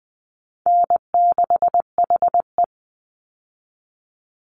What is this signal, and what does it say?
Morse code, heard as a keyed tone, about 20 wpm, 700 Hz. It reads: N6HE
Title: Call Signs - 20wpm